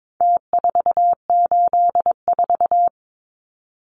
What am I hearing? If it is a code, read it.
Morse code, heard as a keyed tone, about 22 wpm, 700 Hz. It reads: T484
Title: Numbers - Contest Style - 4-Digit 22wpm